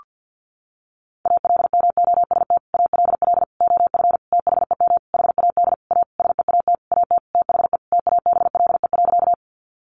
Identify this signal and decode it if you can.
Morse code, heard as a keyed tone, about 50 wpm, 700 Hz. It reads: ALMOST ALL OF THEM HAD A SEAT AT THE TABLE.
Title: Sentences from Top 1500 Words 50wpm